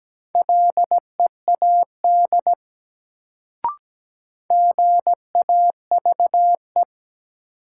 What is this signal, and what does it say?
Morse code, heard as a keyed tone, about 17 wpm, 700 Hz. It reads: LEAD GAVE
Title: Sentences from Top 500 Words - New Word Drill - 17wpm